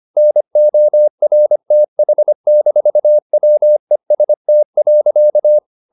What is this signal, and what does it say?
Morse code, heard as a keyed tone, about 25 wpm, 600 Hz. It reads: NORTH-WEST.
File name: the old man and the sea part 1 25wpm